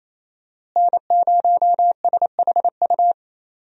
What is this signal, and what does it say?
Morse code, heard as a keyed tone, about 28 wpm, 700 Hz. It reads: N0SHU
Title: USA Call Signs - 5 Characters - Repeated 2x - 28wpm